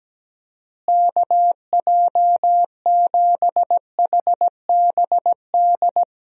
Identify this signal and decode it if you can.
Morse code, heard as a keyed tone, about 17 wpm, 700 Hz. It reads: KJ7HBD
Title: USA Call Signs - 6 Characters - Repeated 2x - 17wpm